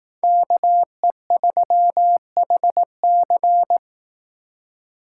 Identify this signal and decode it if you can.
Morse code, heard as a keyed tone, about 18 wpm, 700 Hz. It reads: KE3HC
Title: Speed Racing - USA 2x2 Call Signs 15wpm